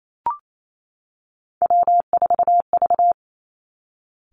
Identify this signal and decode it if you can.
Morse code, heard as a keyed tone, about 28 wpm, 700 Hz. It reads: W4V
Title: Three Challenging Letter-Number Combinations 28wpm